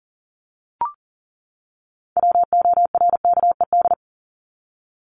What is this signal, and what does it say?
Morse code, heard as a keyed tone, about 40 wpm, 700 Hz. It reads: WORKED
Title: ED Suffix - Top 100 Words with Base Words 40wpm